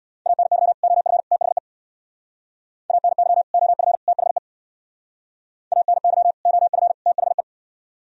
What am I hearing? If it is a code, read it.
Morse code, heard as a keyed tone, about 62 wpm, 700 Hz. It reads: ANY OF THE ANY OF THE ANY OF THE
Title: Speed-Racing - Sets of 3 Words 50wpm